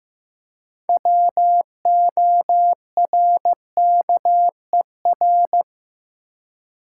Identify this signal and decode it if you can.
Morse code, heard as a keyed tone, about 15 wpm, 700 Hz. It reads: WORKER
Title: Six Letter Words - 15wpm